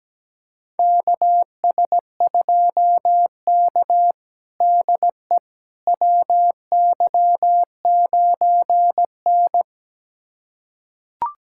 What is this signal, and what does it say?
Morse code, heard as a keyed tone, about 17 wpm, 700 Hz. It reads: KS2K DE WY9N
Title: USA Callsign DE Callsign - 4 Characters - 17wpm